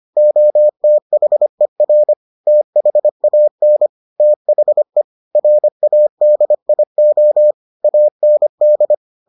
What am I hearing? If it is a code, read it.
Morse code, heard as a keyed tone, about 25 wpm, 600 Hz. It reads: OTHER THAN THE RADIO AND